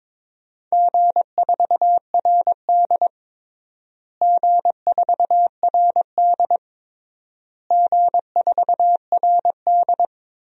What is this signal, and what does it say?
Morse code, heard as a keyed tone, about 22 wpm, 700 Hz. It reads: G4RD G4RD G4RD
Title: DXCC-100 Call Signs - 4 Characters - Repeated 3x - 22wpm